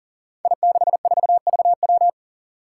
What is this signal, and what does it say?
Morse code, heard as a keyed tone, about 40 wpm, 700 Hz. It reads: I64VW
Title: Five Challenging Letter-Number Combinations 40wpm